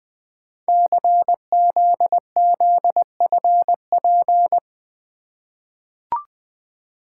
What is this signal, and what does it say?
Morse code, heard as a keyed tone, about 20 wpm, 700 Hz. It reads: CZZFP
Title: Code Groups - 5 Letter:Numbers - 20wpm